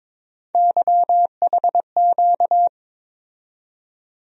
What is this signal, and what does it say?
Morse code, heard as a keyed tone, about 21 wpm, 700 Hz. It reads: YHQ
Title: USA Call Sign Postfixes - 3 Letters 22wpm